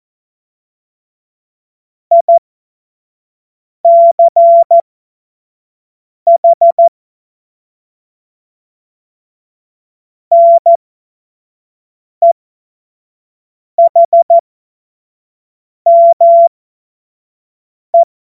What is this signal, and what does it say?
Morse code, heard as a keyed tone, about 14 wpm, spaced as for 5 wpm, 680 Hz. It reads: ICH NEHME
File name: Morsekurs / Witze / Witz 20. Juni 2023 25 F70